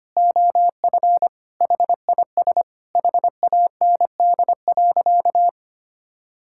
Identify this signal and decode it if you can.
Morse code, heard as a keyed tone, about 25 wpm, 700 Hz. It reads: OF HIS HAND.
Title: Sentences from Top 1000 Words - Review Part 6 - 25wpm